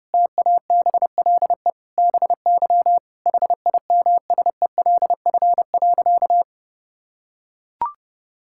Morse code, heard as a keyed tone, about 32 wpm, 700 Hz. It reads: TABLE BY HIMSELF.
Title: Sentences from Top 1600 Words 30wpm